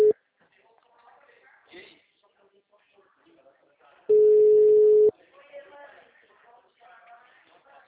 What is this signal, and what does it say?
Ringback tone (Europe, 425 Hz), with speech.